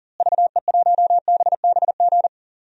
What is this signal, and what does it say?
Morse code, heard as a keyed tone, about 37 wpm, 700 Hz. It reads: VE1BBG
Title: DXCC-100 Call Signs - 6 Characters - Repeated 3x - 40wpm